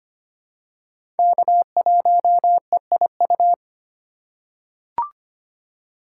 Morse code, heard as a keyed tone, about 25 wpm, 700 Hz. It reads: K1EIU
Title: USA Call Signs - 5 Characters - 2x Character Spacing - 25wpm